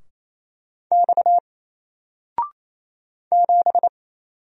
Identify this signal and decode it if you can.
Morse code, heard as a keyed tone, about 28 wpm, 700 Hz. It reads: X 7
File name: Single-Letter-Number-28wpm